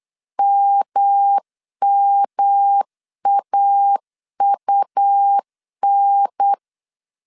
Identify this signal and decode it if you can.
Morse code, heard as a keyed tone, about 8 wpm, 800 Hz. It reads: MMAUN
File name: CW / DET / 1 3 1500C S A 8wpm 4229